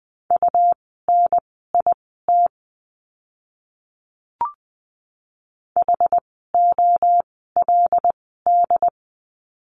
Morse code, heard as a keyed tone, about 20 wpm, spaced as for 14 wpm, 700 Hz. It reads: UNIT HOLD
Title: 101 Lesson 17f - Letter u - Words 20wpm